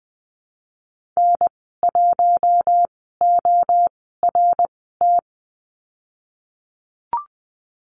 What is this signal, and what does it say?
Morse code, heard as a keyed tone, about 20 wpm, spaced as for 15 wpm, 700 Hz. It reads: N1ORT
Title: 055 Lesson 10g - Letter r - Call Signs 20wpm